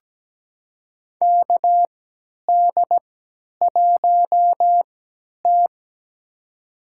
Morse code, heard as a keyed tone, about 17 wpm, spaced as for 10 wpm, 700 Hz. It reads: KD1T
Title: USA Call Signs - 4 Characters - 3x Character Spacing - 17wpm